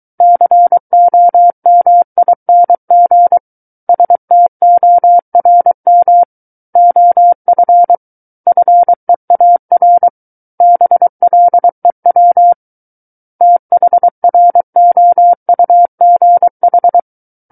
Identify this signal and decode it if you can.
Morse code, heard as a keyed tone, about 23 wpm, 700 Hz. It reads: COMING STORM OF FEAR BLEW THROUGH
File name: Book One / 14-Chapter 14 - 23 WPM